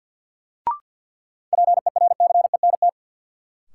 Morse code, heard as a keyed tone, about 48 wpm, 700 Hz. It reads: PERCENT
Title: Sets of 2 Words - New Word Drill - 50wpm